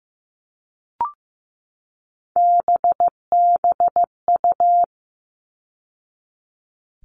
Morse code, heard as a keyed tone, about 15 wpm, 700 Hz. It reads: BBU
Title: All Cut Numbers - 3-Digit 15wpm